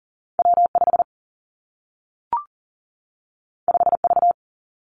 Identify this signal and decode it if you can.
Morse code, heard as a keyed tone, about 41 wpm, 700 Hz. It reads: W5 5V
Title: Two Challenging Letter-Number Combinations 40wpm